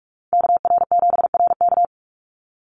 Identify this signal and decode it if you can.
Morse code, heard as a keyed tone, about 44 wpm, 700 Hz. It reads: KR7RK
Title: Call Signs Repeated 2x - 45wpm